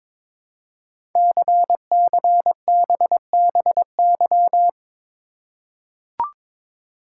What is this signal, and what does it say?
Morse code, heard as a keyed tone, about 22 wpm, 700 Hz. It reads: CCBBY
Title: Code Groups - 5 Letters - 22wpm